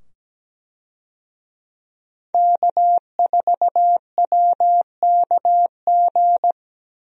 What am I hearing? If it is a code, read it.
Morse code, heard as a keyed tone, about 17 wpm, 700 Hz. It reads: K4WKG
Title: USA Call Signs - 5 Characters - 17wpm